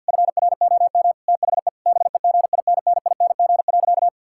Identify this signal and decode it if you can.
Morse code, heard as a keyed tone, about 50 wpm, 700 Hz. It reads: FROM THE BEGINNING.
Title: Sentences from Top 1000 Words - Review Part 5 - 50wpm